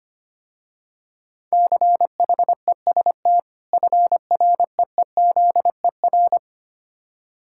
Morse code, heard as a keyed tone, about 26 wpm, 700 Hz. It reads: CHEST FREEZER
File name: practice / 25wpm / Difficult Words - Speed-Racing with Hints - 25wpm